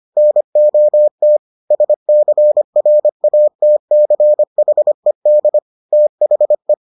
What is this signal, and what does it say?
Morse code, heard as a keyed tone, about 25 wpm, 600 Hz. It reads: NOT SCRATCHED THE